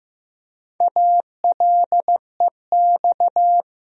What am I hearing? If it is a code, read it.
Morse code, heard as a keyed tone, about 15 wpm, 700 Hz. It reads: ALEX
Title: USA Common Names - Max 5 Letters - 15wpm